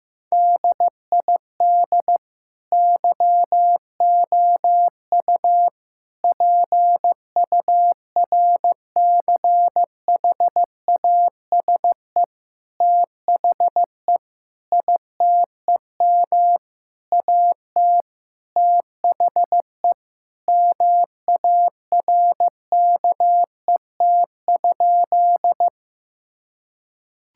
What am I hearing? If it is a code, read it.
Morse code, heard as a keyed tone, about 15 wpm, 700 Hz. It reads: DID YOU PURCHASE THE ITEM AT THE MARKET?
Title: Sentences from Top 1000 Words - Review Part 6 - 15wpm